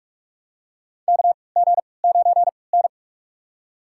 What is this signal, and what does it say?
Morse code, heard as a keyed tone, about 46 wpm, spaced as for 26 wpm, 700 Hz. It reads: KG9N
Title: USA Call Signs - 4 Characters - 3x Character Spacing - 45wpm